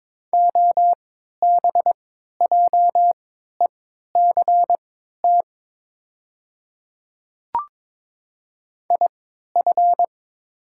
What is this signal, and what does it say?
Morse code, heard as a keyed tone, about 22 wpm, spaced as for 12 wpm, 700 Hz. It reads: OBJECT IF
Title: 3x Character Spacing - Top 500 Words - 22wpm